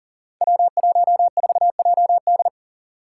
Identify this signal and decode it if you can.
Morse code, heard as a keyed tone, about 41 wpm, 700 Hz. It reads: W14JD
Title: Five Challenging Letter-Number Combinations 40wpm